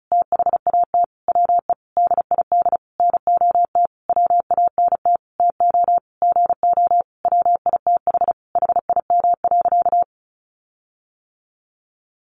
Morse code, heard as a keyed tone, about 34 wpm, 700 Hz. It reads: THAT WE DID NOT WANT TO GO WITH HIM.
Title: Sentences from Top 1300 Words 35wpm